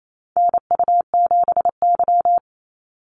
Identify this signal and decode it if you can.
Morse code, heard as a keyed tone, about 28 wpm, 700 Hz. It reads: NU7Y
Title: NA SOTA Activator Callsigns 28wpm